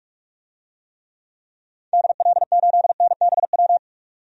Morse code, heard as a keyed tone, about 42 wpm, 700 Hz. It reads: DL8NBW
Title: DXCC-100 Call Signs - 6 Characters - 45wpm